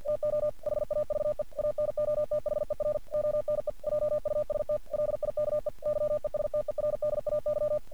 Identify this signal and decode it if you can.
Morse code, heard as a keyed tone, about 45 wpm, 600 Hz. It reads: TO HAVE ANOTHER ONE JUST LIKE YESTERDAY
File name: archived / CRSnet